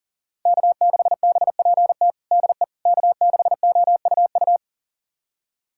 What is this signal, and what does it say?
Morse code, heard as a keyed tone, about 40 wpm, 700 Hz. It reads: K6BPT DE K6OUU
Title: USA Callsign DE Callsign - 5 Characters - 40wpm